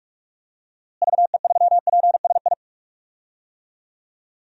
Morse code, heard as a keyed tone, about 41 wpm, 700 Hz. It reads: VE3PSI
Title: DXCC-100 Call Signs - 6 Characters - Repeated 2x - 45wpm